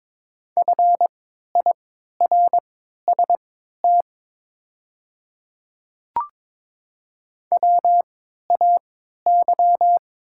Morse code, heard as a keyed tone, about 22 wpm, spaced as for 12 wpm, 700 Hz. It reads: FIRST WAY
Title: 3x Character Spacing - Top 300 Words - 22wpm